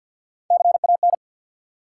Morse code, heard as a keyed tone, about 48 wpm, 700 Hz. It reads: CAN
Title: Three Letter Words - 50wpm